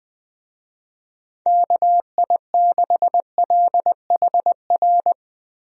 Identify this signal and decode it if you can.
Morse code, heard as a keyed tone, about 20 wpm, 700 Hz. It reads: KI6LHR
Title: USA Call Signs - 6 Characters - Repeated 3x - 20wpm